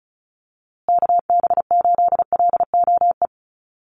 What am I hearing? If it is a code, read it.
Morse code, heard as a keyed tone, about 34 wpm, 700 Hz. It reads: KB8LOE
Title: USA Call Signs Repeated 2x 35wpm